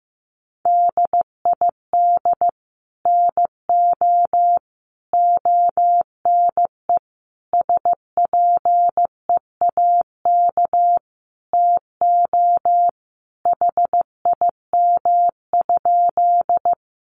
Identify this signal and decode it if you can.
Morse code, heard as a keyed tone, about 15 wpm, 700 Hz. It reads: DID NO ONE SPEAK TO HIM?
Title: Sentences from Top 1100 Words 15wpm